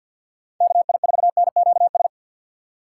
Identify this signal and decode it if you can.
Morse code, heard as a keyed tone, about 48 wpm, 700 Hz. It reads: KI4NQS
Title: USA Call Signs - 6 Characters - 3x Character Spacing - 50wpm